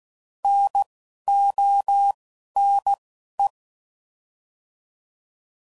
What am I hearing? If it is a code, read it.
Morse code, heard as a keyed tone, about 16 wpm, 800 Hz. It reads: N O N E
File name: Prefixes / By First Letter / Ns